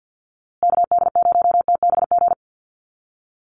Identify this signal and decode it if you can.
Morse code, heard as a keyed tone, about 50 wpm, 700 Hz. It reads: KD0TBG
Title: USA Call Signs Repeated 3x 50wpm